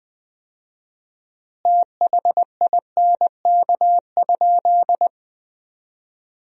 Morse code, heard as a keyed tone, about 20 wpm, 700 Hz. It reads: THINK?
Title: 4x Word-Spacing - Sentences from Top 200 Words - 20wpm